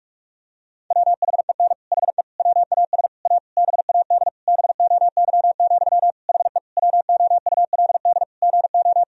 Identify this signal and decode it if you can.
Morse code, heard as a keyed tone, about 45 wpm, 700 Hz. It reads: WHEN HE WAS A BAD BOY, HE WOULD GO
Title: Sentences from Top 1000 Words - Review Part 3 - 45wpm